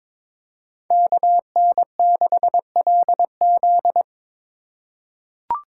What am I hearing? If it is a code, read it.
Morse code, heard as a keyed tone, about 22 wpm, 700 Hz. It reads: KN6LZ
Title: USA Call Signs - 5 Characters - 2x Character Spacing - 22wpm